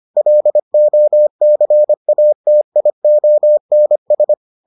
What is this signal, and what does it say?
Morse code, heard as a keyed tone, about 25 wpm, 600 Hz. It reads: LOCATIONS